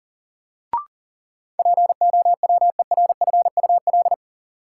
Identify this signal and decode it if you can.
Morse code, heard as a keyed tone, about 40 wpm, 700 Hz. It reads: POWERFUL
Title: Sets of 5 Words Encore - New Word Drill - 40wpm